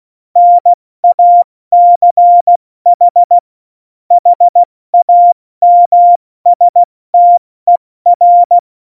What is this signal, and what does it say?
Morse code, heard as a keyed tone, about 16 wpm, 700 Hz. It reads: NACH HAMSTER
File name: Morsekurs / Witze / Witz 17.Juli 2018, 70 F80